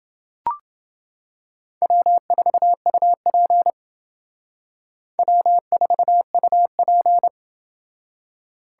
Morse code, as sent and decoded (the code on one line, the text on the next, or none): .-- ....- ..- .--. / .-- ....- ..- .--.
W4UP W4UP